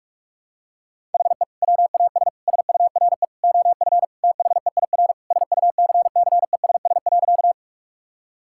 Morse code, heard as a keyed tone, about 46 wpm, 700 Hz. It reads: HE WAS SURE OF THEIR SUCCESS.
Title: Sentences from Top 1000 Words - Review Part 6 - 45wpm